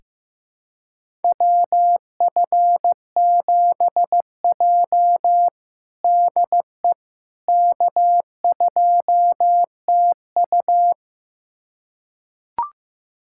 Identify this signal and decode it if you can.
Morse code, heard as a keyed tone, about 15 wpm, 700 Hz. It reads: WF7J DE K2TU
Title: USA Callsign DE Callsign - 4 Characters - 15wpm